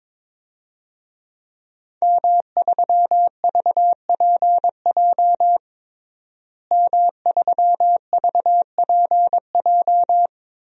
Morse code, heard as a keyed tone, about 22 wpm, 700 Hz. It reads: M3VPJ M3VPJ
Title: DXCC-100 Call Signs - 5 Characters - Repeated 2x - 22wpm